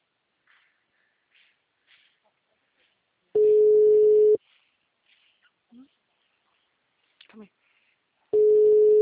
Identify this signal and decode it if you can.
Ringback tone (Europe, 425 Hz), with speech.